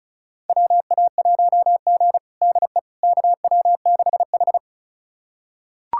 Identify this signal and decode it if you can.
Morse code, heard as a keyed tone, about 36 wpm, 700 Hz. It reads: WA1G DE KW6H
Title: USA Callsign DE Callsign - 4 Characters - 35wpm